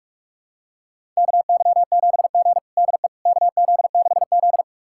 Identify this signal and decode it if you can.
Morse code, heard as a keyed tone, about 42 wpm, 700 Hz. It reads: KY7G DE KZ6Z
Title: USA Callsign DE Callsign - 4 Characters - 45wpm